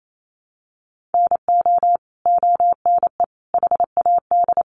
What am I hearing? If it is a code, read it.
Morse code, heard as a keyed tone, about 28 wpm, 700 Hz. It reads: NO ONE HAD
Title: Sentences from Top 1400 Words 28wpm